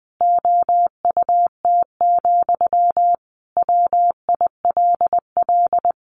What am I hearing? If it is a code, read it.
Morse code, heard as a keyed tone, about 20 wpm, 700 Hz. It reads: OUT, WILL
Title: Sentences from Top 300 Words 20wpm